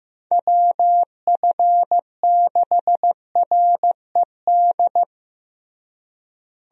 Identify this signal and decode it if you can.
Morse code, heard as a keyed tone, about 15 wpm, 700 Hz. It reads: WF6RED
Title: USA Call Signs - 6 Characters - Repeated 3x - 15wpm